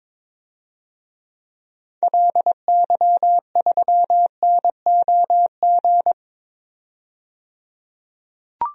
Morse code, heard as a keyed tone, about 22 wpm, 700 Hz. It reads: LY3NOG
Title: DXCC-100 Call Signs - 6 Characters - 22wpm